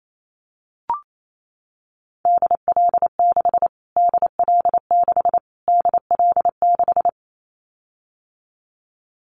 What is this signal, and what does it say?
Morse code, heard as a keyed tone, about 28 wpm, 700 Hz. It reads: DL6 DL6 DL6
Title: Callsign Prefixes for Top 25 DXCC - Repeated 3x - 28wpm